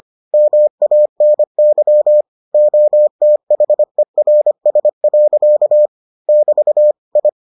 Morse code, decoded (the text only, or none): MANY OTHERS. <BT> I